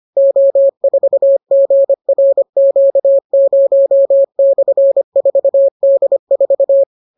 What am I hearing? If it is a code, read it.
Morse code, heard as a keyed tone, about 25 wpm, 550 Hz. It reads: O4GRQ0/4D4